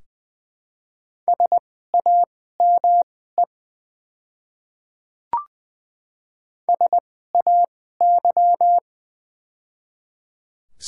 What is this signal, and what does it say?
Morse code, heard as a keyed tone, about 20 wpm, spaced as for 14 wpm, 700 Hz. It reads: SAME SAY
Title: 2x Character Spacing - Top 200 Words - 20wpm